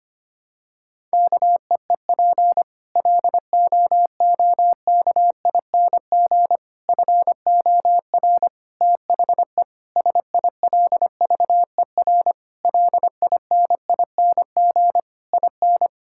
Morse code, heard as a keyed tone, about 25 wpm, 700 Hz. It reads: KEEP LOOKING FOR THE SILVER LINING IN
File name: practice / 25wpm / Difficult Sentences with Hints - 25wpm